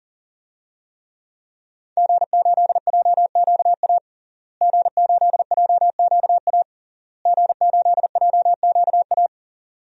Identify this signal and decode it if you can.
Morse code, heard as a keyed tone, about 40 wpm, 700 Hz. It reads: G8JQA G8JQA G8JQA
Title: DXCC-100 Call Signs - 5 Characters - Repeated 3x - 40wpm